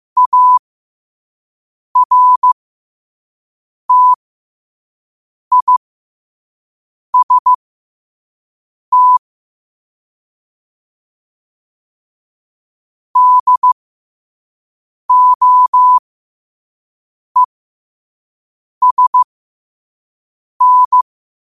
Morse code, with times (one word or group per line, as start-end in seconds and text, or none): 0.2-9.2 ARTIST
13.2-21.0 DOESN